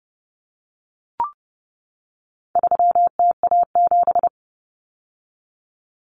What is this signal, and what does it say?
Morse code, heard as a keyed tone, about 30 wpm, 700 Hz. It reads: 3TA7
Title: Common Cut Numbers - 4-Digit 30wpm